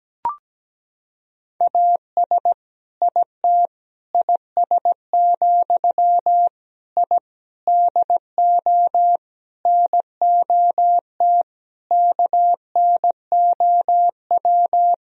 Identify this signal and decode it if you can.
Morse code, heard as a keyed tone, about 17 wpm, 700 Hz. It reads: AS IT IS, I DO NOT KNOW
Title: Sentences from Top 1000 Words - Review Part 2 - 17wpm